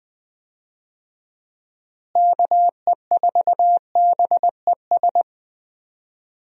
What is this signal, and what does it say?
Morse code, heard as a keyed tone, about 20 wpm, 700 Hz. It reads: KE4BES
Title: USA Call Signs - 6 Characters - Repeated 2x - 20wpm